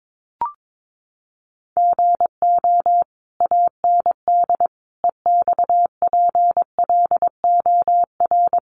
Morse code, heard as a keyed tone, about 22 wpm, 700 Hz. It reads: GO AND EXPLOR
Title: Sentences from Top 1100 Words 22wpm